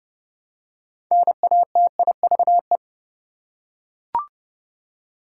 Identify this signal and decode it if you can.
Morse code, heard as a keyed tone, about 29 wpm, 700 Hz. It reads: NATIVE
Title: Six Letter Words - 30wpm